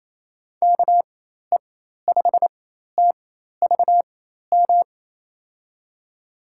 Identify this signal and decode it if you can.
Morse code, heard as a keyed tone, about 27 wpm, spaced as for 13 wpm, 700 Hz. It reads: KE5TVM
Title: USA Call Signs - 6 Characters - 4x Character Spacing - 28wpm